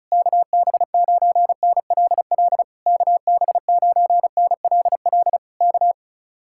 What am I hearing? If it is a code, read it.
Morse code, heard as a keyed tone, about 34 wpm, 700 Hz. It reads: KB9NLL KB9NLL K
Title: Speed-Racing Common QSO Phrases - 30wpm